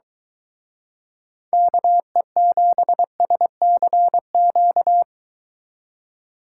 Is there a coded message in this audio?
Yes: Morse code from 1.5 to 5.0 s, heard as a keyed tone, about 23 wpm, 700 Hz. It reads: KE7SCQ